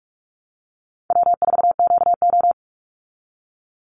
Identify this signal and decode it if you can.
Morse code, heard as a keyed tone, about 44 wpm, 700 Hz. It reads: W4QO
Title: Easy Call Signs Repeated 2x - 45wpm